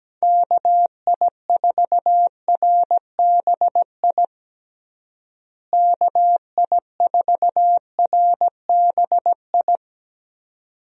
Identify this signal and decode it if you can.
Morse code, heard as a keyed tone, about 17 wpm, 700 Hz. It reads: KI4RBI KI4RBI
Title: USA Call Signs - 6 Characters - Repeated 2x - 17wpm